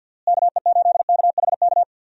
Morse code, heard as a keyed tone, about 46 wpm, 700 Hz. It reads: KE8KHK
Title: USA Call Signs - 6 Characters - Repeated 3x - 50wpm